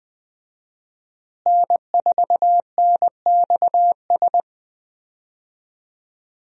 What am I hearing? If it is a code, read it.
Morse code, heard as a keyed tone, about 21 wpm, 700 Hz. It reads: N4NXS
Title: USA Call Signs - 5 Characters - Repeated 2x - 20wpm